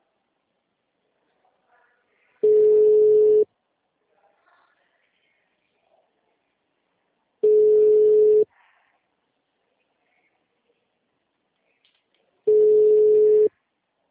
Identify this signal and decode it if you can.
Ringback tone (Europe, 425 Hz).